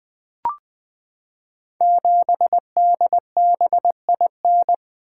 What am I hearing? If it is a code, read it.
Morse code, heard as a keyed tone, about 20 wpm, 700 Hz. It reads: 7DBIN
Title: Code Groups - 5 Letter:Numbers - 20wpm